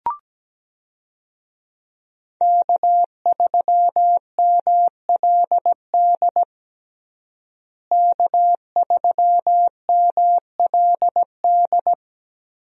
Morse code, heard as a keyed tone, about 17 wpm, 700 Hz. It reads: K3MLD K3MLD
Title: USA Call Signs - 5 Characters - Repeated 2x - 17wpm